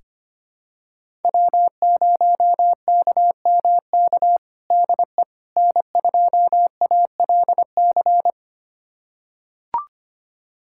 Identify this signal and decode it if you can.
Morse code, heard as a keyed tone, about 25 wpm, 700 Hz. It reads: W0KMK DE N2ALC
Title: USA Callsign DE Callsign - 5 Characters - 25wpm